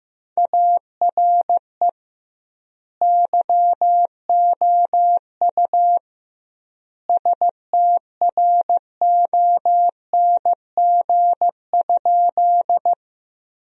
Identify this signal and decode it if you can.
Morse code, heard as a keyed tone, about 15 wpm, 700 Hz. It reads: ARE YOU STRONG?
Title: 2x Word-Spacing - Sentences from Top 500 Words - 15wpm